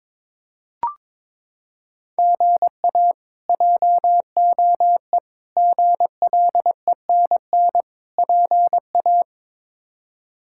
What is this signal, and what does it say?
Morse code, heard as a keyed tone, about 22 wpm, 700 Hz. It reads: GA JOE GLENN PA
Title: K1USN SST QSO Practice - 22wpm